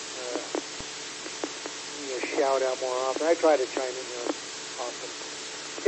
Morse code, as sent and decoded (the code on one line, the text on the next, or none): none